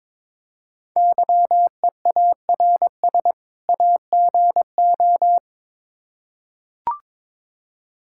Morse code, heard as a keyed tone, about 23 wpm, 700 Hz. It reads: YEARS AGO
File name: Build A Phrase - 2-Word Phrases - 22wpm